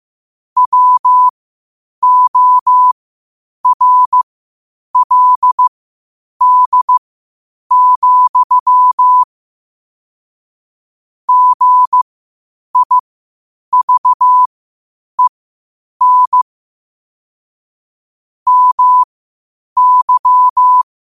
Morse code, heard as a keyed tone, about 15 wpm, spaced as for 8 wpm, 1000 Hz. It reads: WORLD, GIVEN MY
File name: qotdfiles / 8wpm